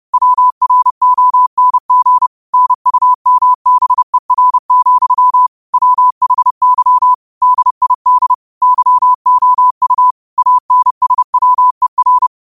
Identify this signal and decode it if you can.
Morse code, heard as a keyed tone, about 30 wpm, 1000 Hz. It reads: WRONG NUMBER, WHY DID YOU ANSWER